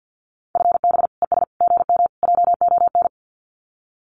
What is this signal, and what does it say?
Morse code, heard as a keyed tone, about 50 wpm, 700 Hz. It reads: FB ES GM JON
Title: Common QSO Phrases Encore 50wpm